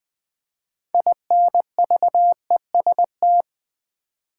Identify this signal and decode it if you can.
Morse code, heard as a keyed tone, about 20 wpm, 700 Hz. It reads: INVEST
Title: Sentences from Top 1100 Words - New Word Drill Alt v2 - 20wpm